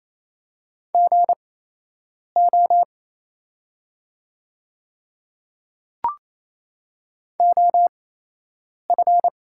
Morse code, heard as a keyed tone, about 27 wpm, spaced as for 8 wpm, 700 Hz. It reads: GO OF
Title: 8x Character Spacing - Top 100 Words - 28wpm